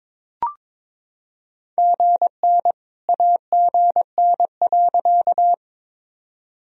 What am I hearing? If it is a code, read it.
Morse code, heard as a keyed tone, about 22 wpm, 700 Hz. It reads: GN AGN.
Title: Sentences from Top 300 Words and Abbreviations 22wpm